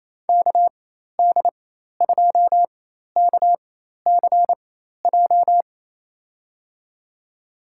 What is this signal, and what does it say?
Morse code, heard as a keyed tone, about 28 wpm, spaced as for 13 wpm, 700 Hz. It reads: KD2KCJ
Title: USA Call Signs - 4x Character Spacing - 28wpm